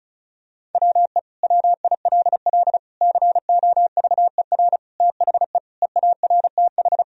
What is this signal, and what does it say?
Morse code, heard as a keyed tone, about 34 wpm, 700 Hz. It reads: WE WILL COVER THE EARTH
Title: Sentences from Top 1000 Words - Review Part 1 - 35wpm